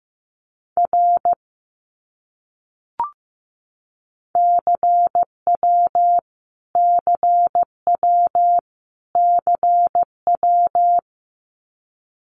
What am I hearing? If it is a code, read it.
Morse code, heard as a keyed tone, about 15 wpm, 700 Hz. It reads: R CW CW CW
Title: Abbreviations Repeated 3x - 15wpm